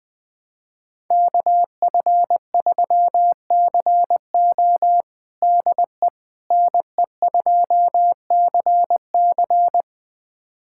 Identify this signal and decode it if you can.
Morse code, heard as a keyed tone, about 20 wpm, 700 Hz. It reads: KF3CO DE NE2CC
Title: USA Callsign DE Callsign 20wpm